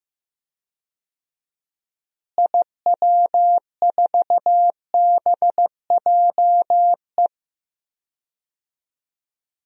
Morse code, heard as a keyed tone, about 15 wpm, 700 Hz. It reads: IW4BJE
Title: DXCC-100 Call Signs - 6 Characters - Repeated 2x - 15wpm